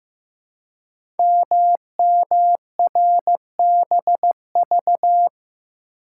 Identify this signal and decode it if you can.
Morse code, heard as a keyed tone, about 15 wpm, 700 Hz. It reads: MMRBV
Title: Code Groups - 5 Letters - 15wpm